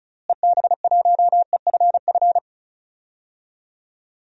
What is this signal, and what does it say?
Morse code, heard as a keyed tone, about 36 wpm, 700 Hz. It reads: EB1EFF
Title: DXCC-100 Call Signs - 6 Characters - Repeated 3x - 35wpm